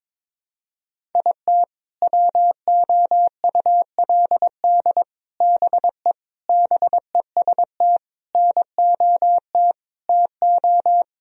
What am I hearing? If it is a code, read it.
Morse code, heard as a keyed tone, about 22 wpm, 700 Hz. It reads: IT WOULD BE BEST NOT TO
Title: Sentences from Top 1000 Words - Review Part 1 - 22wpm